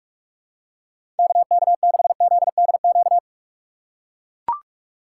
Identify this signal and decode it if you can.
Morse code, heard as a keyed tone, about 41 wpm, 700 Hz. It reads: KK6ZDQ
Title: USA Call Signs - 6 Characters - 2x Character Spacing - 45wpm